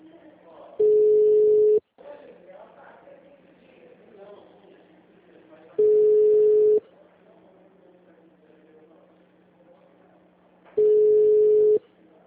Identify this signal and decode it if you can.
Ringback tone (Europe, 425 Hz), with speech.
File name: out- -2002- - - .7